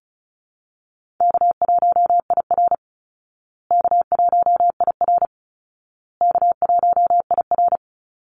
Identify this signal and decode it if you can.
Morse code, heard as a keyed tone, about 34 wpm, 700 Hz. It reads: K1IR K1IR K1IR
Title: Easy Call Signs Repeated 3x - 35wpm